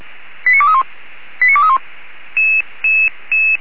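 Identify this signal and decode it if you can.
Five-tone selective call (ZVEI), heard as quick runs of five tones, tones of about 70 ms. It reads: 89231 89231
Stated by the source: ZVEI five-tone selective call, received on SDR.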